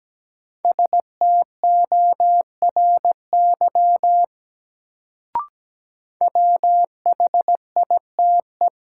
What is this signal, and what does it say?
Morse code, heard as a keyed tone, about 17 wpm, 700 Hz. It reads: STORY WHITE
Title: Sentences from Top 300 Words - New Word Drill - 17wpm